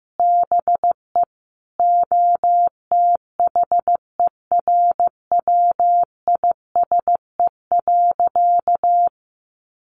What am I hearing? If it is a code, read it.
Morse code, heard as a keyed tone, about 15 wpm, 700 Hz. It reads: BE OTHERWISE.
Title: Sentences from Top 1200 Words 15wpm